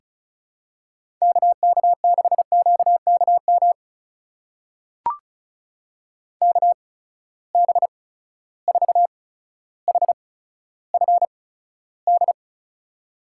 Morse code, heard as a keyed tone, about 36 wpm, spaced as for 15 wpm, 700 Hz. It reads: KK6QKM KB4HFD
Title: USA Call Signs - 6 Characters - 8x Character Spacing - 35wpm